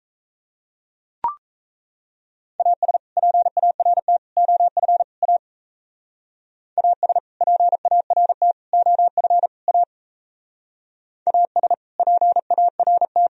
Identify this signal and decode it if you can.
Morse code, heard as a keyed tone, about 38 wpm, 700 Hz. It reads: AS PART OF A AS PART OF A AS PART